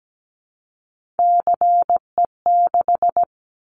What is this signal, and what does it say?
Morse code, heard as a keyed tone, about 17 wpm, 700 Hz. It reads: CE6
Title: Callsign Prefixes for Top 100 DXCC - 17wpm